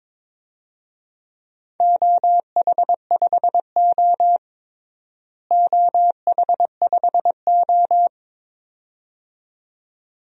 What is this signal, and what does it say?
Morse code, heard as a keyed tone, about 22 wpm, 700 Hz. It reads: OH5O OH5O
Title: DXCC-100 Call Signs - 4 Characters - Repeated 2x - 22wpm